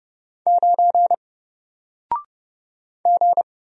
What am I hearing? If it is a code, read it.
Morse code, heard as a keyed tone, about 30 wpm, 700 Hz. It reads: 9 G
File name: Single-Letter-Number-30wpm